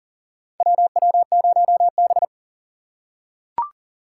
Morse code, heard as a keyed tone, about 40 wpm, 700 Hz. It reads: WW0B
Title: USA Call Signs - 4 Characters - 4x Character Spacing - 40wpm